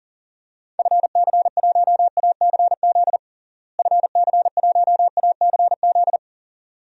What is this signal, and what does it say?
Morse code, heard as a keyed tone, about 40 wpm, 700 Hz. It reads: FC1ACZ FC1ACZ
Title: DXCC-100 Call Signs - 6 Characters - Repeated 2x - 40wpm